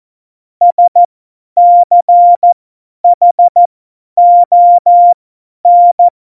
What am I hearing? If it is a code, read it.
Morse code, heard as a keyed tone, about 14 wpm, spaced as for 10 wpm, 700 Hz. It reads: SCHON
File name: Morsekurs / Witze / Witz 20. Oktober 2020, 50 F70